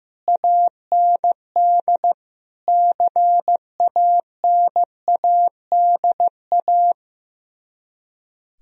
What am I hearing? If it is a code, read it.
Morse code, heard as a keyed tone, about 15 wpm, 700 Hz. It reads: AND CANADA
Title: Speed-Racing - Sets of 5 Words 15wpm